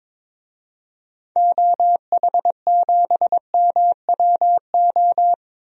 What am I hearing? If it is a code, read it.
Morse code, heard as a keyed tone, about 22 wpm, 700 Hz. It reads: OH7MWO
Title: DXCC-100 Call Signs - 6 Characters - 22wpm